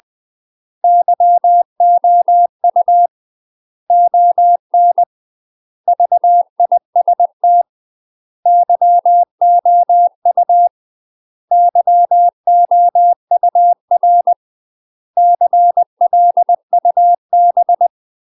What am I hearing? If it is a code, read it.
Morse code, heard as a keyed tone, about 20 wpm, 700 Hz. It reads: YOU ON VIST YOU YOUR CLUB
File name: UK MASTER QSOs TRAINING/20WPM